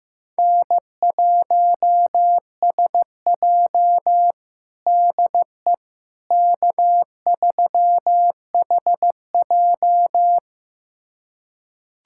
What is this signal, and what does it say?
Morse code, heard as a keyed tone, about 16 wpm, 700 Hz. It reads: N1SJ DE K3HJ
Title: USA Callsign DE Callsign - 4 Characters - 15wpm